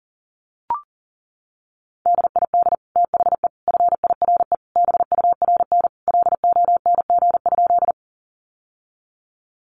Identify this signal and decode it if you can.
Morse code, heard as a keyed tone, about 40 wpm, 700 Hz. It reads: DID THE FIRE BURN LONG?
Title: Sentences from Top 1300 Words 40wpm